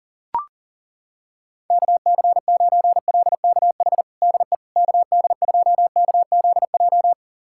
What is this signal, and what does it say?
Morse code, heard as a keyed tone, about 40 wpm, 700 Hz. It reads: KC9LKH DE KD2KZJ
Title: USA Callsign DE Callsign 40wpm